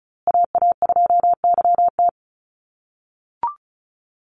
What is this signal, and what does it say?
Morse code, heard as a keyed tone, about 34 wpm, 700 Hz. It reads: AA2YT
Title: USA Call Signs Repeated 3x 35wpm